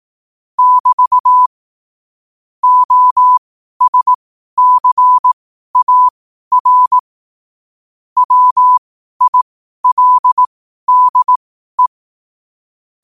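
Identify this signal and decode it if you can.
Morse code, heard as a keyed tone, about 18 wpm, spaced as for 12 wpm, 1000 Hz. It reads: <BT> OSCAR WILDE
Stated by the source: Morse code quotes at 12 words per minute, by ai4qr.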